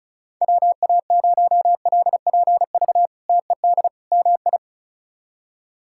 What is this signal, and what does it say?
Morse code, heard as a keyed tone, about 34 wpm, 700 Hz. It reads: WA0LPV TED MI
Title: K1USN SST QSO Practice - 35wpm